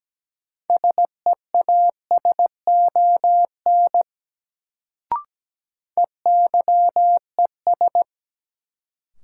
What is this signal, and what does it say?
Morse code, heard as a keyed tone, about 17 wpm, 700 Hz. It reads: SEASON EYES
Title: Sets of 5 Words - New Word Drill - 17wpm